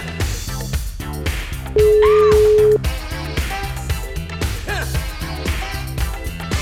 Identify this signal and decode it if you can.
Ringback tone (Europe, 425 Hz).